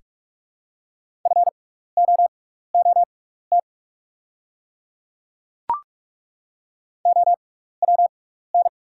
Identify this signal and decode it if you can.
Morse code, heard as a keyed tone, about 43 wpm, spaced as for 15 wpm, 690 Hz. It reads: FOOT OWN
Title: 6x Character Spacing - Top 500 Words - 45wpm